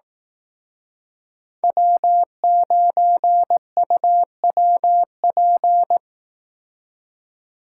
Morse code, heard as a keyed tone, about 18 wpm, 700 Hz. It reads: W9UWP
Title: Speed Racing - USA 1x3 Call Signs 15wpm